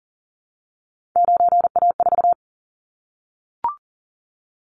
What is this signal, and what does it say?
Morse code, heard as a keyed tone, about 40 wpm, 700 Hz. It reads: 9A4
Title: Callsign Prefixes for Top 25 DXCC - 40wpm